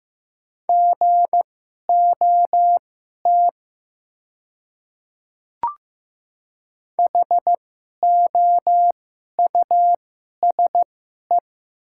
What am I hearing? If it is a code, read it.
Morse code, heard as a keyed tone, about 15 wpm, spaced as for 11 wpm, 700 Hz. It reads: GOT HOUSE
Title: 2x Character Spacing - Top 300 Words - 15wpm